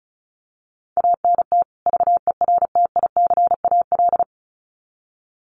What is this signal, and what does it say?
Morse code, heard as a keyed tone, about 34 wpm, 700 Hz. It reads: ANT VERTICAL
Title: Common QSO Phrases 35wpm